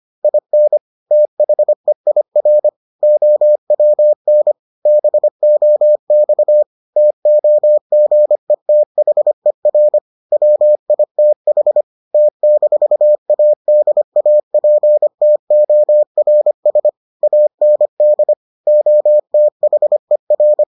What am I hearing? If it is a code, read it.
Morse code, heard as a keyed tone, about 25 wpm, 600 Hz. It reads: IN THEIR OWN BOX TOGETHER WITH T-ADAPTORS AND OTHER